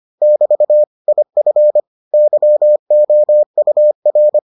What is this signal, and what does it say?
Morse code, heard as a keyed tone, about 25 wpm, 600 Hz. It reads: <BT> IF YOUR